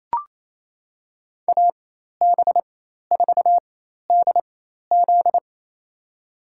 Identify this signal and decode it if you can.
Morse code, heard as a keyed tone, about 27 wpm, spaced as for 13 wpm, 700 Hz. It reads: AB4DZ
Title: USA Call Signs - 4x Character Spacing - 28wpm